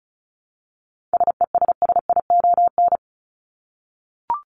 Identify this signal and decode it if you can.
Morse code, heard as a keyed tone, about 34 wpm, 700 Hz. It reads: SESSION
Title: Top 1501-1750 Words 35wpm